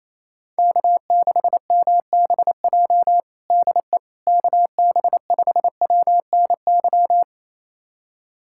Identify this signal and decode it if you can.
Morse code, heard as a keyed tone, about 28 wpm, 700 Hz. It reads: K6MBJ DE KB5WNY
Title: USA Callsign DE Callsign 28wpm